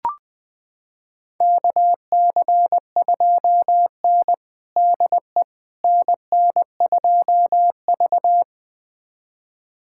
Morse code, heard as a keyed tone, about 20 wpm, 700 Hz. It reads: KC2N DE NN2V
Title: USA Callsign DE Callsign - 4 Characters - 20wpm